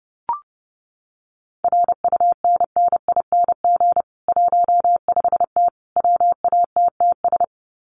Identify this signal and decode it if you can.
Morse code, heard as a keyed tone, about 30 wpm, 700 Hz. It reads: RUNNING 15T WATTS
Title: Common QSO Phrases 30wpm